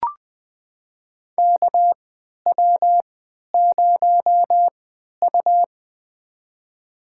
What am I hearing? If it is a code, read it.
Morse code, heard as a keyed tone, about 21 wpm, spaced as for 12 wpm, 700 Hz. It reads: KW0U
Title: USA Call Signs - 4 Characters - 3x Character Spacing - 20wpm